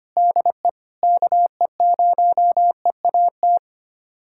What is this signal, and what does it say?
Morse code, heard as a keyed tone, about 25 wpm, 700 Hz. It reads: DE KE0EAT
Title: USA Callsign DE Callsign 25wpm